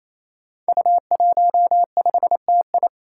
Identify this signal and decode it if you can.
Morse code, heard as a keyed tone, about 28 wpm, 700 Hz. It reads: U15TI
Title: Five Challenging Letter-Number Combinations 28wpm